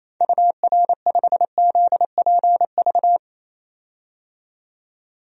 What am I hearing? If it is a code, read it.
Morse code, heard as a keyed tone, about 28 wpm, 700 Hz. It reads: UR5ZPV
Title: DXCC-100 Call Signs - 6 Characters - 28wpm